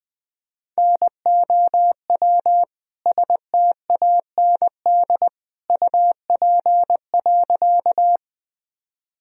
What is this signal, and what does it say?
Morse code, heard as a keyed tone, about 21 wpm, 700 Hz. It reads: NOW STAND UP.
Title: Sentences from Top 1000 Words - Review Part 1 - 20wpm